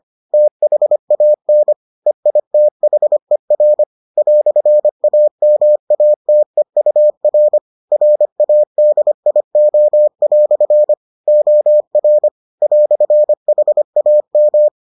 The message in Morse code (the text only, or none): THAN EITHER "AMATEUR RADIO" OR "HAM